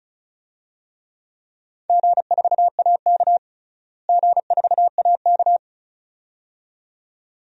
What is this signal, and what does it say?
Morse code, heard as a keyed tone, about 34 wpm, 700 Hz. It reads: G4AK G4AK
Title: DXCC-100 Call Signs - 4 Characters - Repeated 2x - 35wpm